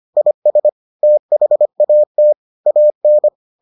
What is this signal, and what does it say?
Morse code, heard as a keyed tone, about 25 wpm, 600 Hz. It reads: IS THAT AN